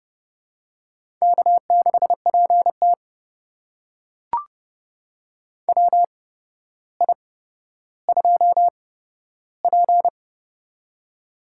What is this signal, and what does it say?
Morse code, heard as a keyed tone, about 30 wpm, spaced as for 13 wpm, 700 Hz. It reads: K6PT WI2P
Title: USA Call Signs - 4 Characters - 8x Character Spacing - 30wpm